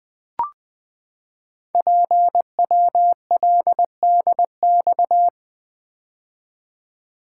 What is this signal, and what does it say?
Morse code, heard as a keyed tone, about 20 wpm, 700 Hz. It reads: PWLDX
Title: Code Groups - 5 Letter:Numbers - 20wpm